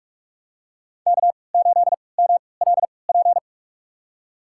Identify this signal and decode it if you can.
Morse code, heard as a keyed tone, about 46 wpm, spaced as for 25 wpm, 700 Hz. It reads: K8MLP
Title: USA Call Signs - 5 Characters - 3x Character Spacing - 45wpm